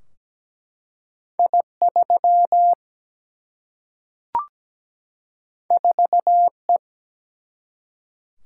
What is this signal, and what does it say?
Morse code, heard as a keyed tone, about 17 wpm, 700 Hz. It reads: I3 4E
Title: Single-Letter Single-Number 17wpm